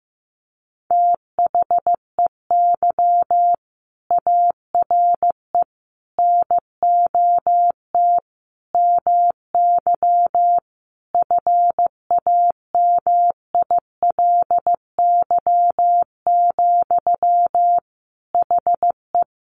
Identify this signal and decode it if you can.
Morse code, heard as a keyed tone, about 15 wpm, 700 Hz. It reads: THEY ARE NOT MY FAMILY, HE
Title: Sentences from Top 1100 Words 15wpm